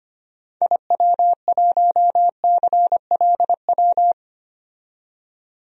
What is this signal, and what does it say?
Morse code, heard as a keyed tone, about 26 wpm, 690 Hz. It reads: IW1CLW
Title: DXCC-100 Call Signs - 6 Characters - 25wpm